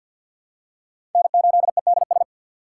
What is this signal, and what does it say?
Morse code, heard as a keyed tone, about 47 wpm, 700 Hz. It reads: N8EDS
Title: USA Call Signs - 5 Characters - 3x Character Spacing - 50wpm